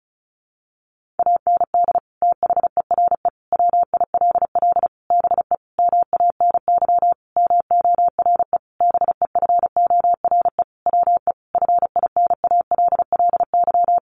Morse code, heard as a keyed tone, about 34 wpm, 700 Hz. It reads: AND THERE WILL BE MANY MORE BEFORE WE FINALLY
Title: Sentences from Top 1500 Words 35wpm